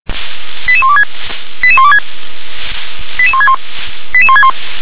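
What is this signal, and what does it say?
Five-tone selective call (ZVEI), heard as quick runs of five tones, tones of about 70 ms. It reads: 88126 88126 88161 88161
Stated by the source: ZVEI five-tone selective call received on SDR.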